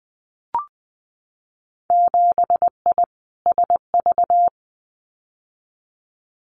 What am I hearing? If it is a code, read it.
Morse code, heard as a keyed tone, about 20 wpm, 700 Hz. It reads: 7I SV
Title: Winter Field Day Exchanges Encore 20wpm